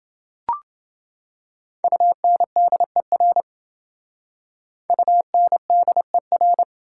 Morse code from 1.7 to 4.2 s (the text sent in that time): UNDER